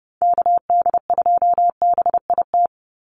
Morse code, heard as a keyed tone, about 30 wpm, 700 Hz. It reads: KD2BIT
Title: USA Call Signs 30wpm